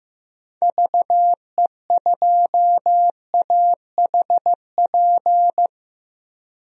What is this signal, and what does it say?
Morse code, heard as a keyed tone, about 15 wpm, 700 Hz. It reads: VE2AHP
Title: DXCC-100 Call Signs - 6 Characters - Repeated 2x - 15wpm